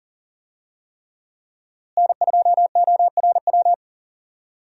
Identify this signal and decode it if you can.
Morse code, heard as a keyed tone, about 40 wpm, 700 Hz. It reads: N2ORW
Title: USA Call Signs - 5 Characters - Repeated 3x - 40wpm